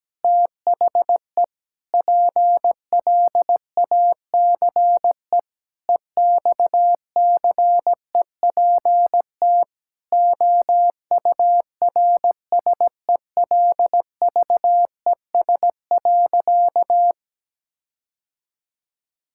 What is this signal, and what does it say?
Morse code, heard as a keyed tone, about 17 wpm, 700 Hz. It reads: THE PLACE EXCEPT OURSELVES.
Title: Sentences from Top 1600 Words 17wpm